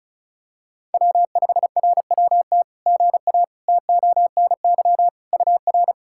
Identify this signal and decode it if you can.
Morse code, heard as a keyed tone, about 36 wpm, 700 Hz. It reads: W5RWT GA TONY UR
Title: SOTA QSO Practice 35wpm